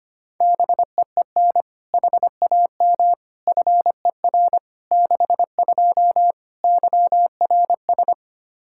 Morse code, heard as a keyed tone, about 25 wpm, 700 Hz. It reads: BEEN HAM FER 62 YRS
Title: K1USN SST QSO Practice 25wpm